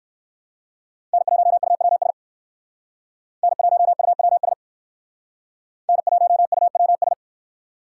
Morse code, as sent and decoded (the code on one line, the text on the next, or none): -. .---- ..- .-- ... / -. .---- ..- .-- ... / -. .---- ..- .-- ...
N1UWS N1UWS N1UWS